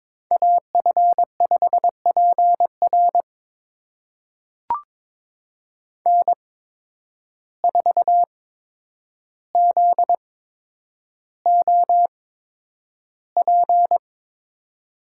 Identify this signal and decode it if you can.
Morse code, heard as a keyed tone, about 22 wpm, spaced as for 9 wpm, 700 Hz. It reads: AF5PR N4ZOP
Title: USA Call Signs - 5 Characters - 8x Character Spacing - 22wpm